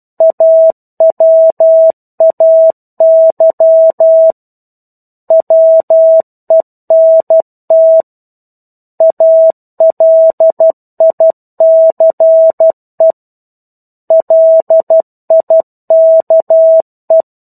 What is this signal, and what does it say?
Morse code, heard as a keyed tone, about 12 wpm, 650 Hz. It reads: AWAY WENT ALICE LIKE
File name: 01-Chapter-01 12 WPM